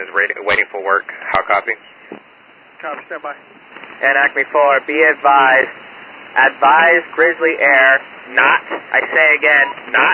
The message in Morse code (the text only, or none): none